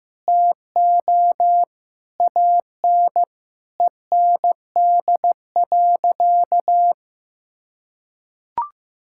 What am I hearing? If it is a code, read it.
Morse code, heard as a keyed tone, about 15 wpm, 700 Hz. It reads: TO AN END.
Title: Sentences from Top 1000 Words - Review Part 5 - 15wpm